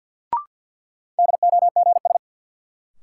Morse code, heard as a keyed tone, about 47 wpm, 700 Hz. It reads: DOGS
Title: Sentences from Top 1600 Words - New Word Drill - 50wpm